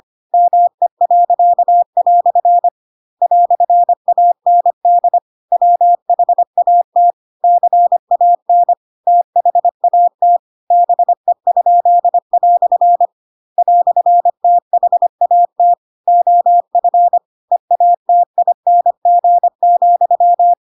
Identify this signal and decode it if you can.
Morse code, heard as a keyed tone, about 25 wpm, 700 Hz. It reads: ME." "AND WHAT CAN THAT BE?" "THAT OF EATING,
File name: Pinocchio / Pinocchio ch4 25 25 ik1hge